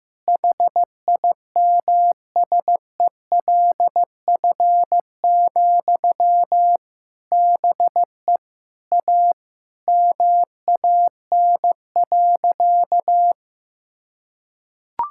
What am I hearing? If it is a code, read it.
Morse code, heard as a keyed tone, about 15 wpm, 700 Hz. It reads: HIMSELF, BE A MAN.
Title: Sentences from Top 1600 Words 15wpm